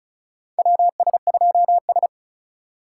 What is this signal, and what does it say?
Morse code, heard as a keyed tone, about 36 wpm, 700 Hz. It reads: WS2S
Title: USA Call Signs - 4 Characters - 2x Character Spacing - 35wpm